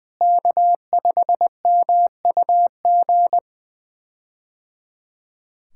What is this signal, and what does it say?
Morse code, heard as a keyed tone, about 20 wpm, 700 Hz. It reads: K5MUG
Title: K1USN SST QSO Practice - 20wpm